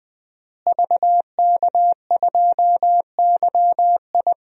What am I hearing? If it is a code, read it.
Morse code, heard as a keyed tone, about 21 wpm, 700 Hz. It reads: VK2YI
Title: DXCC-100 Call Signs - 5 Characters - Repeated 2x - 20wpm